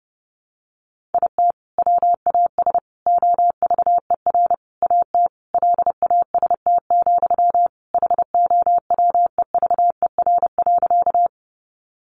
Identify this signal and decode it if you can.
Morse code, heard as a keyed tone, about 30 wpm, 700 Hz. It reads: IT WAS OVER AT LAST, HOWEVER.
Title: Sentences from Top 600 Words 30wpm